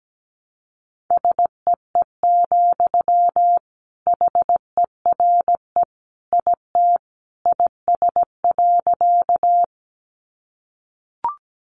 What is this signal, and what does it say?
Morse code, heard as a keyed tone, about 17 wpm, 700 Hz. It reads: SEE, HERE IT IS.
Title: Sentences from Top 200 Words 17wpm